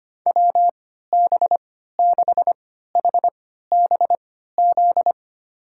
Morse code, heard as a keyed tone, about 25 wpm, spaced as for 14 wpm, 700 Hz. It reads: WB6HBZ
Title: USA Call Signs - 6 Characters - 3x Character Spacing - 25wpm